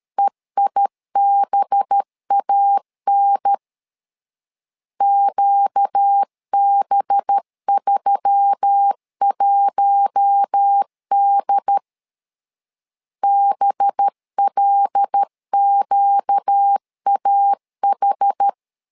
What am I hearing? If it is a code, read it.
Morse code, heard as a keyed tone, about 13 wpm, 800 Hz. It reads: EIBAN QB31D BLQAH
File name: LEZIONI / 5 LBVC / 5 LB 12wpm 1906